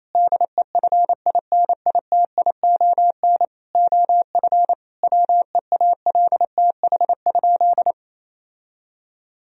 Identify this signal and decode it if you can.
Morse code, heard as a keyed tone, about 28 wpm, 700 Hz. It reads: DEFINITION OF WEALTH?
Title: Sentences from Top 1000 Words - Review Part 4 - 28wpm